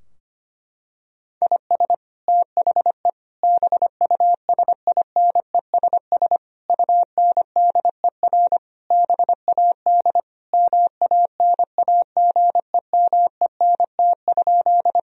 Morse code, heard as a keyed tone, about 25 wpm, 700 Hz. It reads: IS THE BUSINESS UNDER BAD MANAGEMENT?
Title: Sentences from Top 1000 Words - Review Part 4 - 25wpm